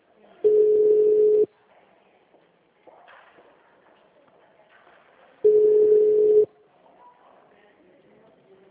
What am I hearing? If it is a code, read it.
Ringback tone (Europe, 425 Hz), with speech.